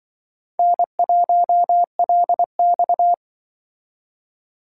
Morse code, heard as a keyed tone, about 25 wpm, 700 Hz. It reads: N1LX